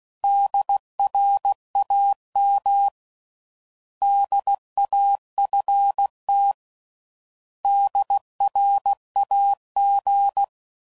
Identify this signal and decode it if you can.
Morse code, heard as a keyed tone, about 16 wpm, 800 Hz. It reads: DRAM DAFT DRAG
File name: Prefixes / By First Letter / Ds